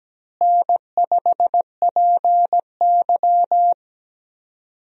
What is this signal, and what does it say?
Morse code, heard as a keyed tone, about 17 wpm, 700 Hz. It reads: N5PY
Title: USA Call Signs - 4 Characters - Repeated 3x - 17wpm